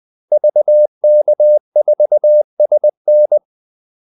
Morse code, heard as a keyed tone, about 20 wpm, 600 Hz. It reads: VK4SN
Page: Kaldesignaler 20 wpm